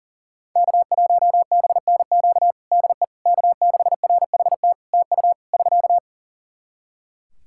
Morse code, heard as a keyed tone, about 40 wpm, 700 Hz. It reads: K1BNQ DE K6RHT TU <SK>
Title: K1USN SST QSO Practice 40wpm